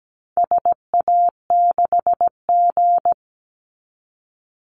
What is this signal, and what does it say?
Morse code, heard as a keyed tone, about 17 wpm, 700 Hz. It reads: SA6G
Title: Call Signs - 17wpm